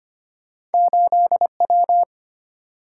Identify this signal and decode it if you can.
Morse code, heard as a keyed tone, about 34 wpm, spaced as for 30 wpm, 700 Hz. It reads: 8W
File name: practice / 25wpm / Speed-Racing - Two Letter-Number Combinations - 25wpm